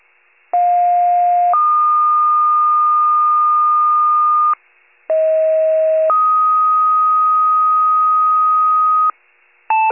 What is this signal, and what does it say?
Two-tone page, heard as a one-second tone, then a three-second tone, about 710 then 1220 Hz, about 640 then 1250 Hz. It is heard from 0.5 to 9.1 s.
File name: pager/38 FIRE/2025-11-11 wcalert